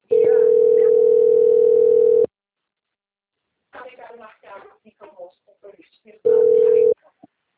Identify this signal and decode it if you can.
Ringback tone (North America), 440 + 480 Hz, with speech.
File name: out- - - - - .36426